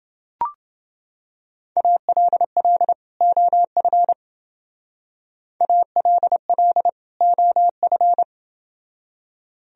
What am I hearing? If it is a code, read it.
Morse code, heard as a keyed tone, about 28 wpm, 700 Hz. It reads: ALL OF ALL OF